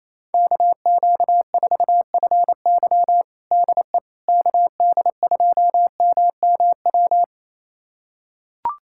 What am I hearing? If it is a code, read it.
Morse code, heard as a keyed tone, about 28 wpm, 700 Hz. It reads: KQ4FY DE KD2MMW
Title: USA Callsign DE Callsign 28wpm